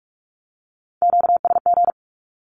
Morse code, heard as a keyed tone, about 44 wpm, 700 Hz. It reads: QSG
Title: Q-Codes 45wpm